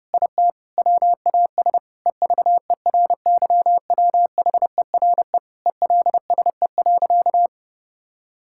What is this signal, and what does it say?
Morse code, heard as a keyed tone, about 30 wpm, 700 Hz. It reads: IT WAS EVERYWHERE ELSE.
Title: Sentences from Top 1600 Words 30wpm